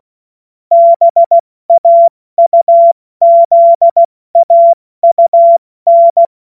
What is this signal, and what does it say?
Morse code, heard as a keyed tone, about 16 wpm, 680 Hz. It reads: BAUZAUN
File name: Morsekurs / Witze / Witz 23. Dezember 2025 70 F80